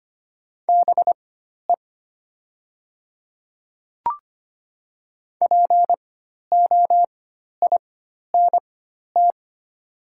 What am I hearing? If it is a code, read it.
Morse code, heard as a keyed tone, about 25 wpm, spaced as for 12 wpm, 700 Hz. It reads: BE POINT
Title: 4x Character Spacing - Top 200 Words - 25wpm